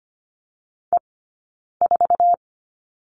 Morse code, heard as a keyed tone, about 25 wpm, 700 Hz. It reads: E 4
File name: course / 25:10wpm / 092 Lesson 16d - Letter c - Two Characters 25wpm